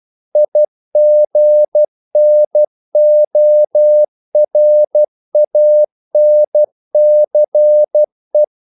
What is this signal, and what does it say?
Morse code, heard as a keyed tone, about 12 wpm, 600 Hz. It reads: IGNORANCE
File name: CW / Practice / EnglishQuotations0112